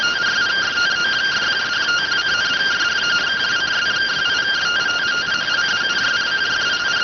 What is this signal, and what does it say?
Two-tone teleprinter signal, about 170 Hz shift.